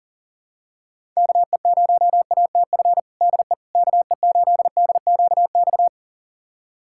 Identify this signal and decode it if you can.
Morse code, heard as a keyed tone, about 40 wpm, 700 Hz. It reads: KE0ATF DE KE8DQX
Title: USA Callsign DE Callsign 40wpm